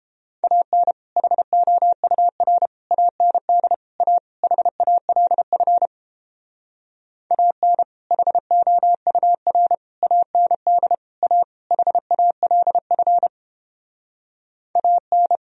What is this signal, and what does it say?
Morse code, heard as a keyed tone, about 29 wpm, 700 Hz. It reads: AN HOUR AND A HALF AN HOUR AND A HALF AN